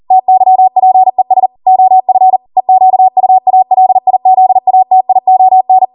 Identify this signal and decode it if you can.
Morse code, heard as a keyed tone, about 40 wpm, 750 Hz. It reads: TYPES OF EQUALIZATION